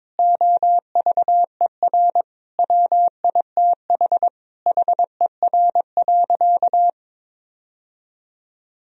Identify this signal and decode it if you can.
Morse code, heard as a keyed tone, about 22 wpm, 700 Hz. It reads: OVER WITH HER.
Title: Sentences from Top 1000 Words - Review Part 2 - 22wpm